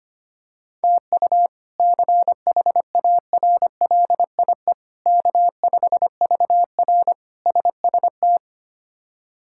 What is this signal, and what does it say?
Morse code, heard as a keyed tone, about 25 wpm, 700 Hz. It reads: TU CHARLIE K5VR SST
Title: K1USN SST QSO Practice - 25wpm